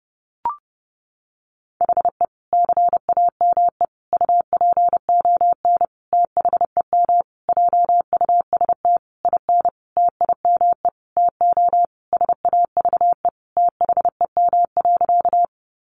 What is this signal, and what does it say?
Morse code, heard as a keyed tone, about 28 wpm, 700 Hz. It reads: HE CAME UPON THEM JUST IN TIME TO SAVE THEM.
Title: Sentences from Top 1100 Words 30wpm